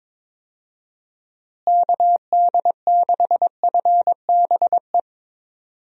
Morse code, heard as a keyed tone, about 22 wpm, 700 Hz. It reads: KD6FBE
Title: USA Call Signs - 6 Characters - Repeated 2x - 22wpm